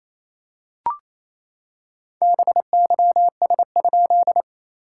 Morse code, heard as a keyed tone, about 27 wpm, 690 Hz. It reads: BYS?
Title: SOTA QSO Practice 28wpm